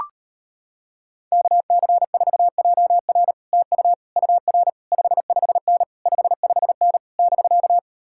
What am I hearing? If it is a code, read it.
Morse code, heard as a keyed tone, about 33 wpm, 700 Hz. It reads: KC4JR TU UR 55N 55N <BK>